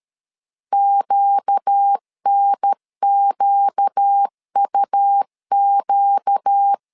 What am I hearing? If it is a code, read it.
Morse code, heard as a keyed tone, about 13 wpm, 800 Hz. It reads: QNQUQ
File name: LEZIONI / 4 Q 12wpm 1833